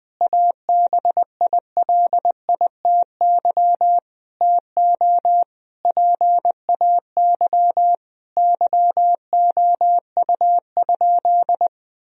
Morse code, heard as a keyed tone, about 20 wpm, 700 Hz. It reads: ABILITY TO PAY YOU?
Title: Sentences from Top 1000 Words - Review Part 3 - 20wpm